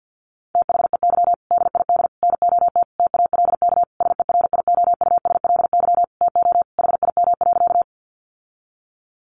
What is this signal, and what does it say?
Morse code, heard as a keyed tone, about 49 wpm, 700 Hz. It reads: THEY DID NOT TALK SERIOUSLY TO HIM.
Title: Sentences from Top 1200 Words 50wpm